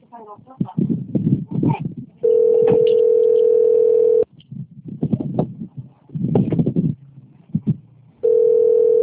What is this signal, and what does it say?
Ringback tone (North America), 440 + 480 Hz, with speech.
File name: out- - - - - .43213